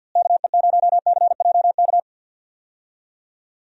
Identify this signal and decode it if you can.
Morse code, heard as a keyed tone, about 51 wpm, 700 Hz. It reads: KE0CJK
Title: USA Call Signs - 6 Characters - Repeated 2x - 50wpm